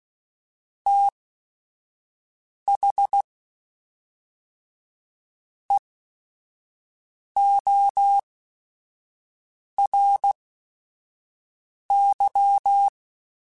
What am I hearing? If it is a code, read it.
Morse code, heard as a keyed tone, about 16 wpm, spaced as for 4 wpm, 800 Hz. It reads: THEORY